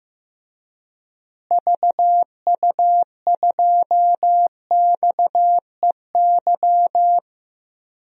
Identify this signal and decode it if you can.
Morse code, heard as a keyed tone, about 15 wpm, 700 Hz. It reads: VU2XEY
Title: DXCC-100 Call Signs - 6 Characters - Repeated 2x - 15wpm